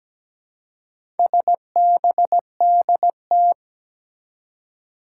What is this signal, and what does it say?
Morse code, heard as a keyed tone, about 17 wpm, 700 Hz. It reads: SBDT
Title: Four Challenging Letter-Number Combinations 17wpm